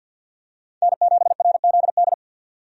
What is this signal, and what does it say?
Morse code, heard as a keyed tone, about 53 wpm, 700 Hz. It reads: N7RZD
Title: USA Call Signs - 5 Characters - 50wpm